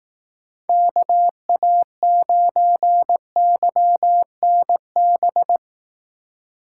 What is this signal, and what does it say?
Morse code, heard as a keyed tone, about 18 wpm, 700 Hz. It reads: KA9YNB
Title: Speed Racing - USA 2x3 Call Signs 15wpm